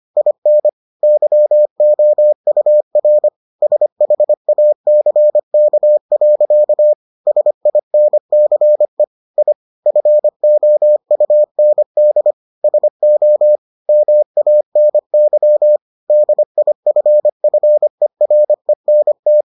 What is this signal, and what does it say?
Morse code, heard as a keyed tone, about 25 wpm, 600 Hz. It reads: IN YOUR SHACK. SINCE I FOUND SO MANY DIFFERENT